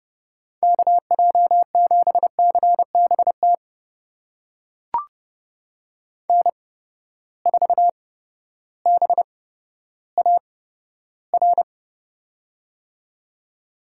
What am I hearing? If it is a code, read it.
Morse code, heard as a keyed tone, about 30 wpm, spaced as for 14 wpm, 700 Hz. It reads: KJ7CBT N4BAR
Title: USA Call Signs - 8x Character Spacing - 30wpm